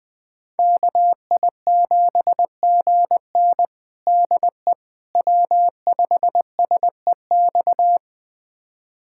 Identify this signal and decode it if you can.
Morse code, heard as a keyed tone, about 20 wpm, 700 Hz. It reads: KI7GN DE W5SEX
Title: USA Callsign DE Callsign - 5 Characters - 20wpm